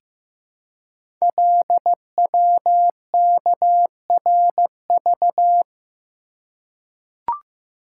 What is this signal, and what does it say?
Morse code, heard as a keyed tone, about 15 wpm, 700 Hz. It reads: LWKRV
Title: Code Groups - 5 Letters - 15wpm